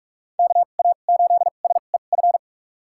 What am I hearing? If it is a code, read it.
Morse code, heard as a keyed tone, about 46 wpm, spaced as for 32 wpm, 700 Hz. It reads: KA8SEF
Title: USA Call Signs - 2x Character Spacing - 45wpm